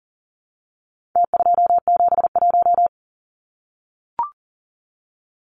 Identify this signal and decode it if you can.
Morse code, heard as a keyed tone, about 40 wpm, 700 Hz. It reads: T271
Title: Numbers - Contest Style - 4-Digit 40wpm